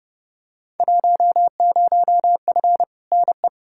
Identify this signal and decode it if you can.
Morse code, heard as a keyed tone, about 29 wpm, 700 Hz. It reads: 10F NE
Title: Field Day Exchanges Encore - 30wpm